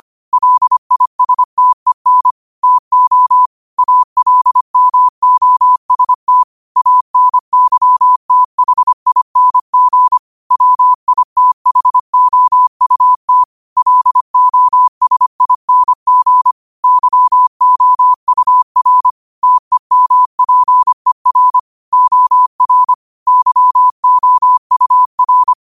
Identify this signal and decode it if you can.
Morse code, heard as a keyed tone, about 25 wpm, 1000 Hz. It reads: LISTEN TO ALMOST ANYTHING WITHOUT LOSING YOUR TEMPER OR YOUR